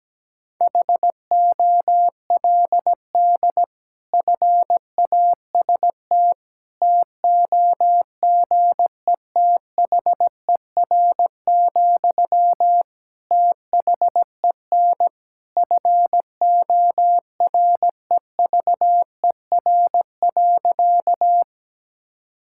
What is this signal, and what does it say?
Morse code, heard as a keyed tone, about 17 wpm, 700 Hz. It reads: HOLD FAST TOGETHER, THEN FOREVER.
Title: Sentences from Top 1600 Words 17wpm